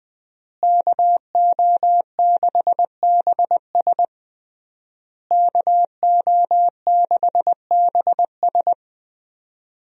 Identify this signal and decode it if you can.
Morse code, heard as a keyed tone, about 20 wpm, 700 Hz. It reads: KO6BS KO6BS
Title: USA Call Signs - 5 Characters - Repeated 3x - 20wpm